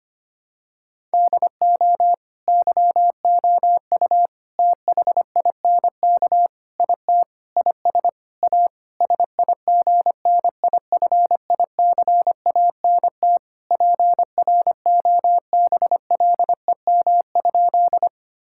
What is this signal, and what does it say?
Morse code, heard as a keyed tone, about 25 wpm, 700 Hz. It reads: DO YOU THINK IT IS A SIGNIFICANT PROBLEM?
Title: Sentences from Top 1000 Words - Review Part 6 - 25wpm